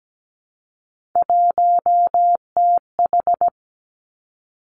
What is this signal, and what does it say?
Morse code, heard as a keyed tone, about 17 wpm, 700 Hz. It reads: 1TH
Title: Three Challenging Letter-Number Combinations 17wpm